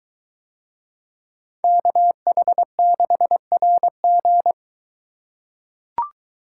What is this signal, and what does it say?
Morse code, heard as a keyed tone, about 22 wpm, 700 Hz. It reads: KH6RG